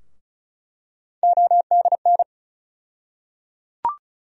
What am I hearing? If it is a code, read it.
Morse code, heard as a keyed tone, about 34 wpm, 700 Hz. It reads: ODN
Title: USA Call Sign Postfixes - 3 Letters 35wpm